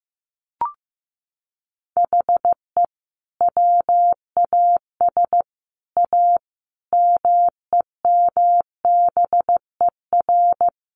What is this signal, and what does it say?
Morse code, heard as a keyed tone, about 15 wpm, 700 Hz. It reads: HE WAS A MEMBER
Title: Sets of 5 Words Encore2 15wpm